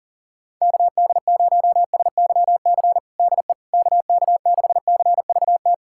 Morse code, heard as a keyed tone, about 37 wpm, 700 Hz. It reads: KD0SYC DE KK6CVT
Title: USA Callsign DE Callsign 40wpm